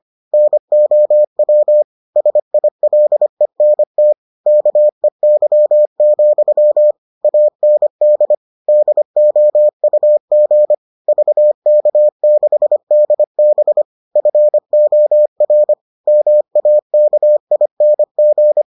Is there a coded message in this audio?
Yes: Morse code from 0.3 to 18.6 s, heard as a keyed tone, about 25 wpm, 600 Hz. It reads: NOW SILENT KEY, AND DOUG VK6DB FOR MAKING